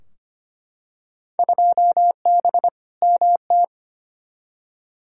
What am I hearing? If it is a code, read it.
Morse code, heard as a keyed tone, about 25 wpm, 700 Hz. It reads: 2B MT
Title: Field Day Exchanges Encore - 25wpm